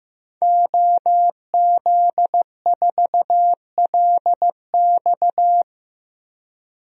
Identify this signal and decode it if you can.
Morse code, heard as a keyed tone, about 16 wpm, 700 Hz. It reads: OZ4LX
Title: DXCC-100 Call Signs - 5 Characters - Repeated 2x - 15wpm